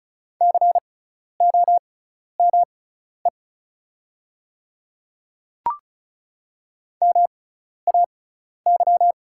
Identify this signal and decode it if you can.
Morse code, heard as a keyed tone, about 34 wpm, spaced as for 12 wpm, 700 Hz. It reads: COME MAY
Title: 6x Character Spacing - Top 100 Words - 35wpm